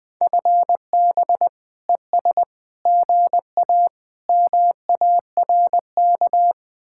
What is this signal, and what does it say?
Morse code, heard as a keyed tone, about 20 wpm, 700 Hz. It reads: FB ES GA MARK
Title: Speed-Racing Common QSO Phrases - 17wpm